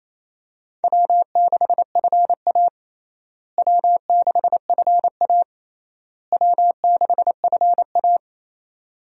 Morse code, heard as a keyed tone, about 27 wpm, 700 Hz. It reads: W6FA W6FA W6FA
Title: USA Call Signs - 4 Characters - Repeated 3x - 28wpm